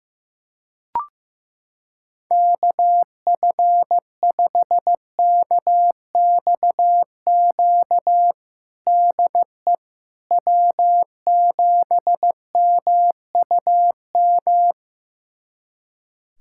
Morse code, heard as a keyed tone, about 15 wpm, 700 Hz. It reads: KF5KXQ DE W7MUM
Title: USA Callsign DE Callsign 15wpm